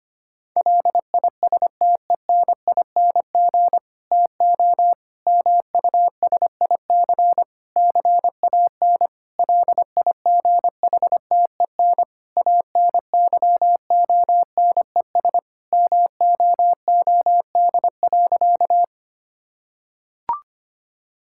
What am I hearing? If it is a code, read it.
Morse code, heard as a keyed tone, about 25 wpm, 700 Hz. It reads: LISTENING TO MUSIC CAN LIGHTEN ANYONES MOOD.
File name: practice / 25wpm / Difficult Sentences with Hints - 25wpm